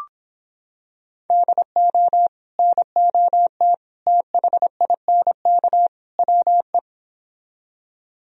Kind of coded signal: Morse code

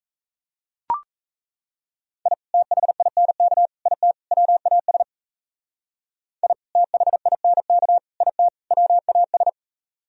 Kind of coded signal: Morse code

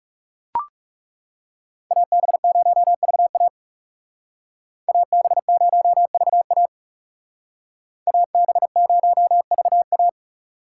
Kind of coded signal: Morse code

